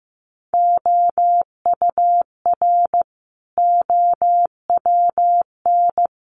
Morse code, heard as a keyed tone, about 15 wpm, 700 Hz. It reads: OUR OWN
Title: Sentences from Top 300 Words 15wpm